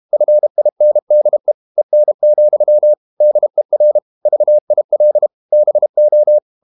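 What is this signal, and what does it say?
Morse code, heard as a keyed tone, about 32 wpm, 600 Hz. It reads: FINDE EN, DER VIL BO
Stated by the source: High-speed QRQ CW at 32 WPM.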